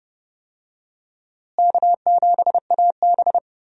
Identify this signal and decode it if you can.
Morse code, heard as a keyed tone, about 30 wpm, 700 Hz. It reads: K7AB
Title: USA Call Signs - 4 Characters - Repeated 2x - 30wpm